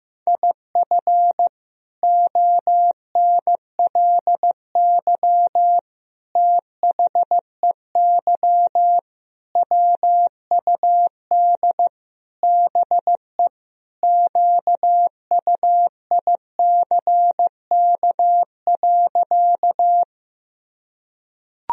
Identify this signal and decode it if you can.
Morse code, heard as a keyed tone, about 15 wpm, 700 Hz. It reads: IF ONLY THEY WUD BE QUICK.
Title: Sentences from Top 500 Words and Abbreviations 15wpm